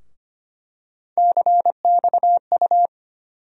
Morse code, heard as a keyed tone, about 25 wpm, 700 Hz. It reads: CXU
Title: USA Call Sign Postfixes - 3 Letters 25wpm